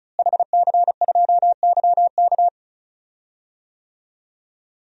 Morse code, heard as a keyed tone, about 36 wpm, 700 Hz. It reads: HC2YK
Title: DXCC-100 Call Signs - 5 Characters - Repeated 2x - 35wpm